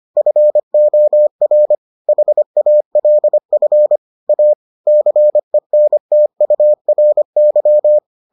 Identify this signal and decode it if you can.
Morse code, heard as a keyed tone, about 25 wpm, 600 Hz. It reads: FOR HALF A CENTURY